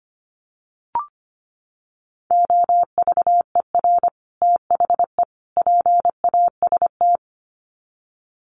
Morse code, heard as a keyed tone, about 25 wpm, 700 Hz. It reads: OVER THE PAST
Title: Sets of 3 Words 25wpm